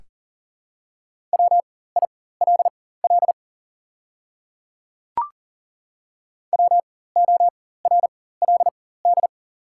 Morse code, heard as a keyed tone, about 40 wpm, spaced as for 19 wpm, 690 Hz. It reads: WILL WORLD
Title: 4x Character Spacing - Top 300 Words - 40wpm